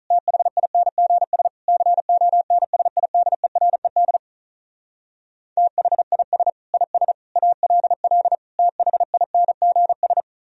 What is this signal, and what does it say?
Morse code, heard as a keyed tone, about 46 wpm, 700 Hz. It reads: THINGS CONSIDERED THIS IS ALL THINGS